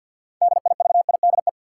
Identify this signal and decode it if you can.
Morse code, heard as a keyed tone, about 52 wpm, 700 Hz. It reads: DIVIDE
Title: Sentences from Top 1600 Words - New Word Drill - 50wpm